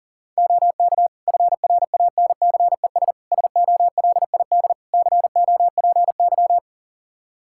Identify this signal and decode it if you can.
Morse code, heard as a keyed tone, about 40 wpm, 700 Hz. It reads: OK FRANCES SOLID COPY
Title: Practice QSO - Exchange 3 - More Info - 40wpm